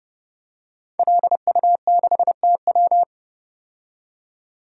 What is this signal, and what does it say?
Morse code, heard as a keyed tone, about 30 wpm, 700 Hz. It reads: LU6TW
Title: DXCC-100 Call Signs - 5 Characters - 30wpm